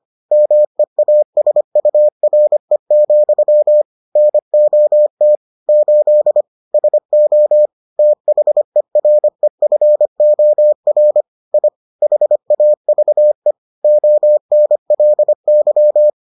Morse code, heard as a keyed tone, about 25 wpm, 600 Hz. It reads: MEASURE, NOT 8 SO THEREFOR I HAVE ONLY